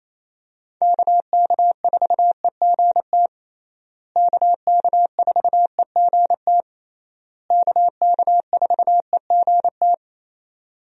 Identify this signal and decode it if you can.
Morse code, heard as a keyed tone, about 27 wpm, 700 Hz. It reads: KK4EGT KK4EGT KK4EGT
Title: USA Call Signs - 6 Characters - Repeated 3x - 28wpm